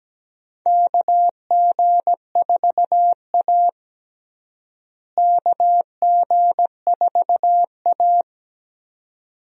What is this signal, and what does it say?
Morse code, heard as a keyed tone, about 17 wpm, 700 Hz. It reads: KG4A KG4A
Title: USA Call Signs - 4 Characters - Repeated 2x - 17wpm